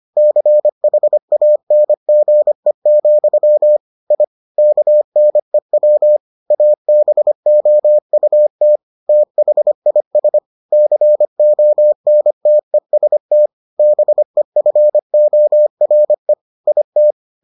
Morse code, heard as a keyed tone, about 25 wpm, 600 Hz. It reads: CHANGE, I KNEW ABOUT THIS CONTEST BEFORE IT